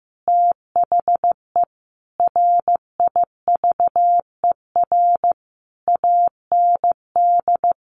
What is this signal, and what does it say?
Morse code, heard as a keyed tone, about 15 wpm, 700 Hz. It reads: THE RIVER AND
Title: Sentences from Top 1200 Words 15wpm